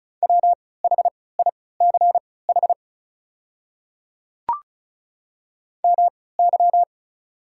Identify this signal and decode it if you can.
Morse code, heard as a keyed tone, about 35 wpm, spaced as for 20 wpm, 690 Hz. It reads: WHICH MY
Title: 3x Character Spacing - Top 300 Words - 35wpm